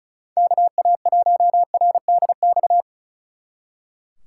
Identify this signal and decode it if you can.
Morse code, heard as a keyed tone, about 33 wpm, 700 Hz. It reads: KA1RDX
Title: Speed Racing - USA 2x3 Call Signs 35wpm